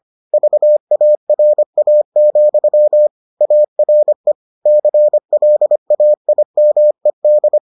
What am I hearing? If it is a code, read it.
Morse code, heard as a keyed tone, about 25 wpm, 600 Hz. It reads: VARA, ARE CLAIMED